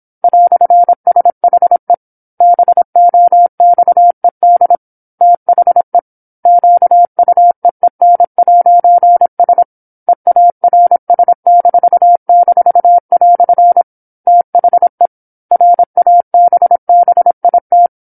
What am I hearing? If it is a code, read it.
Morse code, heard as a keyed tone, about 27 wpm, 700 Hz. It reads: "SHE BOXED THE QUEEN'S EARS--" THE RABBIT
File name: 08-Chapter-08 26 WPM